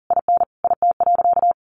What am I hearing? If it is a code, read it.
Morse code, heard as a keyed tone, about 40 wpm, 700 Hz. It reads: IN IT.
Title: Sentences from Top 1400 Words 40wpm